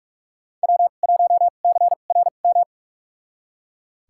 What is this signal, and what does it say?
Morse code, heard as a keyed tone, about 42 wpm, spaced as for 31 wpm, 700 Hz. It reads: W1CRM
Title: USA Call Signs - 5 Characters - 2x Character Spacing - 45wpm